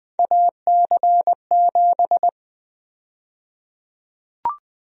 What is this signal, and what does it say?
Morse code, heard as a keyed tone, about 20 wpm, 700 Hz. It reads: AC7
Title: USA Call Sign Prefixes - Repeated 2x - 20wpm